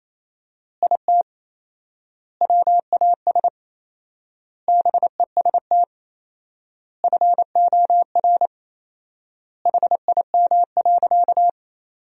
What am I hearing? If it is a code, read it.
Morse code, heard as a keyed tone, about 28 wpm, 700 Hz. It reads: IT WAS BEST FOR HIM.
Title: 4x Word-Spacing - Sentences from Top 500 Words - 28wpm